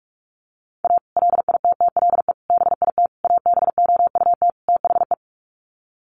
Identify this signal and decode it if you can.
Morse code, heard as a keyed tone, about 44 wpm, 700 Hz. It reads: A LITTLE BIT ABOUT THE
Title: Sets of 5 Words Encore2 45wpm